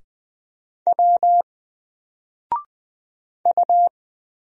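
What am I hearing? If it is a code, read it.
Morse code, heard as a keyed tone, about 20 wpm, 700 Hz. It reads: W U
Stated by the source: Single letter and number Morse drill at 20 wpm, from N4LCD.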